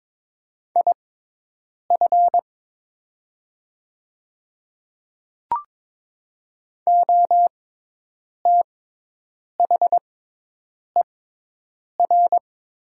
Morse code, heard as a keyed tone, about 22 wpm, spaced as for 8 wpm, 700 Hz. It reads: IF OTHER
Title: 6x Character Spacing - Top 100 Words - 22wpm